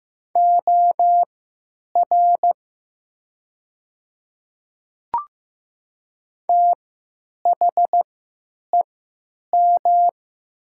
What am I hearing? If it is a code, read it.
Morse code, heard as a keyed tone, about 15 wpm, spaced as for 9 wpm, 700 Hz. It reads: OR THEM
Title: 3x Character Spacing - Top 200 Words - 15wpm